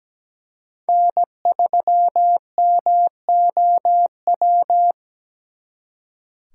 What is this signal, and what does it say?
Morse code, heard as a keyed tone, about 20 wpm, 700 Hz. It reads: N3MOW
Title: Speed Racing - USA 1x3 Call Signs 17wpm